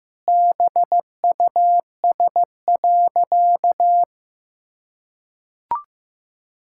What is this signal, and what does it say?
Morse code, heard as a keyed tone, about 16 wpm, 700 Hz. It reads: BUS.
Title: Sentences from Top 1000 Words - Review Part 6 - 15wpm